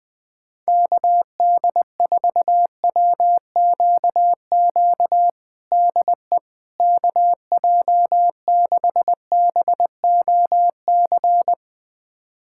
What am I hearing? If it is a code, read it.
Morse code, heard as a keyed tone, about 20 wpm, 700 Hz. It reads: KD4WQQ DE KJ6BOC
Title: USA Callsign DE Callsign - 6 Characters - 20wpm